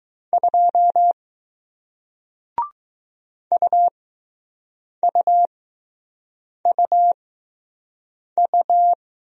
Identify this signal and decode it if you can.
Morse code, heard as a keyed tone, about 20 wpm, spaced as for 6 wpm, 700 Hz. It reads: 2 UUUU